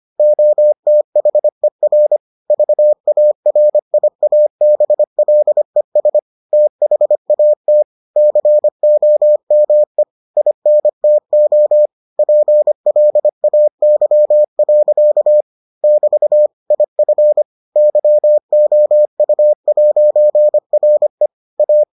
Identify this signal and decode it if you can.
Morse code, heard as a keyed tone, about 25 wpm, 600 Hz. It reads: OTHER VARIABLES THAT COME INTO PLAY. <BT> IF YOU'RE A